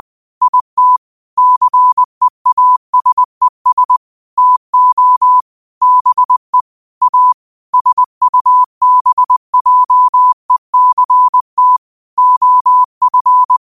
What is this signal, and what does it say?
Morse code, heard as a keyed tone, about 20 wpm, 1000 Hz. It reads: IT CEASES TO BE A SUBJECT OF
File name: qotdfiles / 20wpm